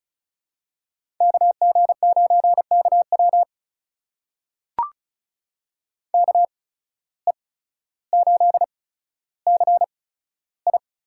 Morse code, heard as a keyed tone, about 36 wpm, spaced as for 15 wpm, 700 Hz. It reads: KG9KW KE8CI
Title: USA Call Signs - 8x Character Spacing - 35wpm